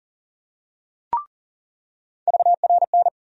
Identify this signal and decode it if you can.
Morse code, heard as a keyed tone, about 40 wpm, 700 Hz. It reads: VRN
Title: USA Call Sign Postfixes - 3 Letters 40wpm